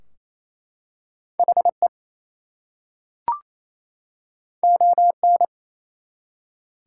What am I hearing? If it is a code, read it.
Morse code, heard as a keyed tone, about 27 wpm, 700 Hz. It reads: HE ON
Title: Two Letter Words - 28wpm